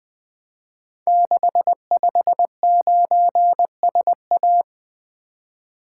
Morse code, heard as a keyed tone, about 20 wpm, 700 Hz. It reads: 659SA
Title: Code Groups - 5 Letter:Numbers - 20wpm